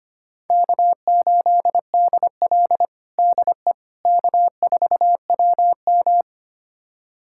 Morse code, heard as a keyed tone, about 25 wpm, 700 Hz. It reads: K8DL DE K4WM
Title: USA Callsign DE Callsign - 4 Characters - 25wpm